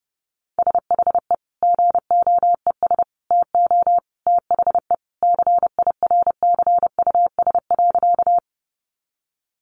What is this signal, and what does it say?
Morse code, heard as a keyed tone, about 28 wpm, 700 Hz. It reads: SHE GOES TO THE CIRCUS.
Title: Easy Phrases 30wpm